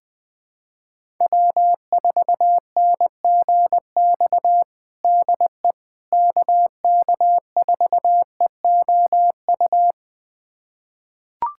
Morse code, heard as a keyed tone, about 21 wpm, 700 Hz. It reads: W4NGX DE KK4EOU
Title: USA Callsign DE Callsign 20wpm